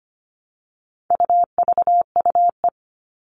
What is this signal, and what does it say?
Morse code, heard as a keyed tone, about 25 wpm, 700 Hz. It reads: UVUE
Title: All Cut Numbers - 4-Digit 25wpm